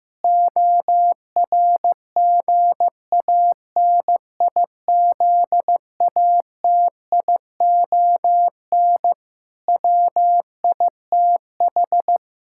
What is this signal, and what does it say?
Morse code, heard as a keyed tone, about 16 wpm, 700 Hz. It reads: ORGANIZATION WITH
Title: Sentences from Top 1000 Words - Review Part 5 - 15wpm